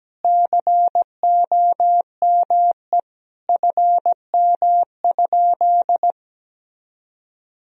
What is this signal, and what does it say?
Morse code, heard as a keyed tone, about 17 wpm, 700 Hz. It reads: COME FM?
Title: Sentences from Top 200 Words and Abbreviations 17wpm